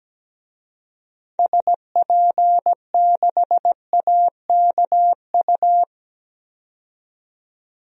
Morse code, heard as a keyed tone, about 17 wpm, 700 Hz. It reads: SP6AKU
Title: DXCC-100 Call Signs - 6 Characters - 17wpm